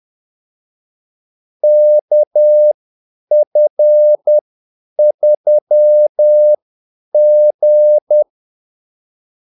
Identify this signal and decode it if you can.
Morse code, heard as a keyed tone, about 10 wpm, spaced as for 8 wpm, 600 Hz. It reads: KF3G
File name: Callsigns-10-8wpm